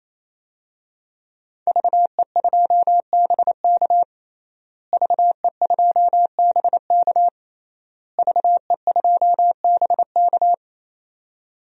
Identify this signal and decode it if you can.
Morse code, heard as a keyed tone, about 28 wpm, 700 Hz. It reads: VE2BK VE2BK VE2BK
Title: DXCC-100 Call Signs - 5 Characters - Repeated 3x - 28wpm